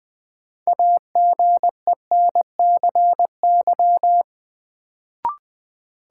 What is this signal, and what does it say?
Morse code, heard as a keyed tone, about 20 wpm, 700 Hz. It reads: AGENCY
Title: Sentences from Top 1300 Words - New Word Drill - 20wpm